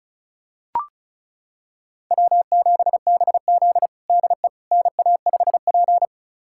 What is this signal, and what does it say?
Morse code, heard as a keyed tone, about 36 wpm, 700 Hz. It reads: W7BZ DE NA5P
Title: USA Callsign DE Callsign - 4 Characters - 35wpm